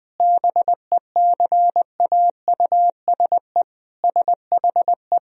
Morse code, heard as a keyed tone, about 21 wpm, 700 Hz. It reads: BECAUSE SHE
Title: Sentences from Top 1000 Words - Review Part 4 - 20wpm